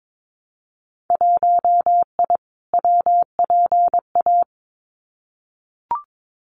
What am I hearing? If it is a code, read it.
Morse code, heard as a keyed tone, about 22 wpm, 700 Hz. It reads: 1I WPA
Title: Winter Field Day Exchanges Encore 22wpm